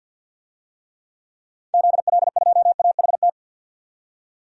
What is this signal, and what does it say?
Morse code, heard as a keyed tone, about 51 wpm, 700 Hz. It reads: ZL2AHT
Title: DXCC-100 Call Signs - 6 Characters - 50wpm